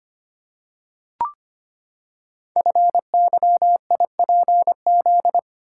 Morse code, heard as a keyed tone, about 25 wpm, 700 Hz. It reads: FYIPZ
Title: Code Groups - 5 Letter:Numbers - 25wpm